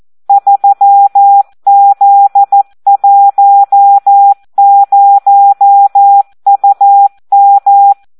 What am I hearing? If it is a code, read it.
Morse code, heard as a keyed tone, about 14 wpm, 800 Hz. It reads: 3Z10UM
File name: Audio practica CW / SD 1 14